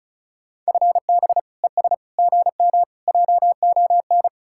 Morse code, heard as a keyed tone, about 44 wpm, 700 Hz. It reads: FB ES GM JON
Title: Speed-Racing Common QSO Phrases - 35wpm